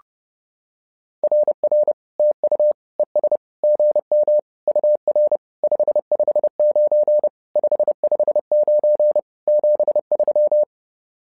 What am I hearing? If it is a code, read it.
Morse code, heard as a keyed tone, about 30 wpm, 600 Hz. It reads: RR TU ES GM UR 559 559 73
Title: SOTA QSO Practice 30wpm